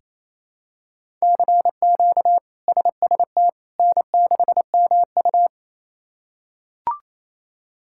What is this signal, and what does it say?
Morse code, heard as a keyed tone, about 28 wpm, 700 Hz. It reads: CQ SST N6MU
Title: K1USN SST QSO Practice - 28wpm